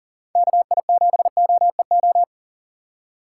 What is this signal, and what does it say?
Morse code, heard as a keyed tone, about 40 wpm, 690 Hz. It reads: KI7OEO
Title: USA Call Signs - 6 Characters - 2x Character Spacing - 40wpm